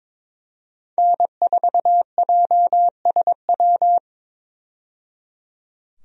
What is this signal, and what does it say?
Morse code, heard as a keyed tone, about 21 wpm, 700 Hz. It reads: N4JSW
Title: USA Call Signs - 5 Characters - 22wpm